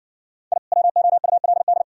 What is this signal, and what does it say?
Morse code, heard as a keyed tone, about 60 wpm, 690 Hz. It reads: I WOULD